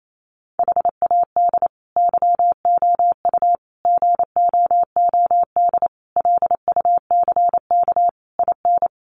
Morse code, heard as a keyed tone, about 28 wpm, 700 Hz. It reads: HAD YOU GOOD LUCK IN
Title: Sentences from Top 1500 Words 28wpm